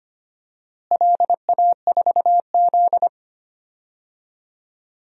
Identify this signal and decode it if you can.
Morse code, heard as a keyed tone, about 25 wpm, 700 Hz. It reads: LA4Z
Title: DXCC-100 Call Signs - 4 Characters - Repeated 2x - 25wpm